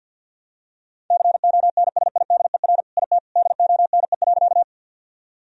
Morse code, heard as a keyed tone, about 50 wpm, 700 Hz. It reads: CONSIDER IT DONE.
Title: Sentences from Top 1000 Words - Review Part 4 - 50wpm